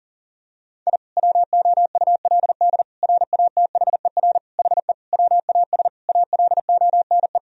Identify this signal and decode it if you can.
Morse code, heard as a keyed tone, about 40 wpm, 700 Hz. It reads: I WOULD RATHER HE WAS ALONE
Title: Sentences from Top 1000 Words - Review Part 4 - 40wpm